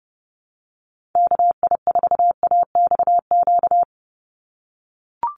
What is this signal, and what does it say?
Morse code, heard as a keyed tone, about 30 wpm, 700 Hz. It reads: KI4AXQ
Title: USA Call Signs 30wpm